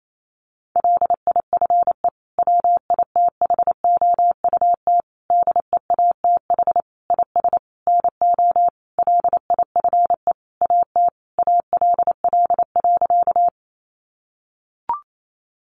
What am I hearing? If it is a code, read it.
Morse code, heard as a keyed tone, about 28 wpm, 700 Hz. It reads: LIFE WITHOUT DEATH IS NO LIFE AT ALL.
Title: Sentences from Top 1100 Words 28wpm